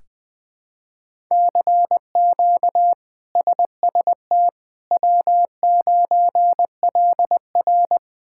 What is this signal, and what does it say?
Morse code, heard as a keyed tone, about 21 wpm, 700 Hz. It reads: CQ SST W9LR
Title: K1USN SST QSO Practice - 20wpm